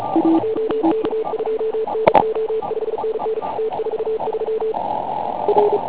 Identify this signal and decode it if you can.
Morse code, heard as a keyed tone, about 34 wpm, 440 Hz. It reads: OK2NO 5NN T43 R
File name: Pile UP CUC204